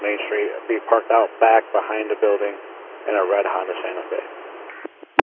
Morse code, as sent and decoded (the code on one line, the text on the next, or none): none